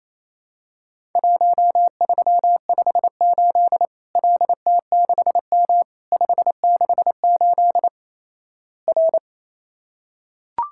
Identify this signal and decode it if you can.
Morse code, heard as a keyed tone, about 27 wpm, 700 Hz. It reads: 1358 LT6M 568 R
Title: Worked All Europe DX Contest - QTC Exchange - 28wpm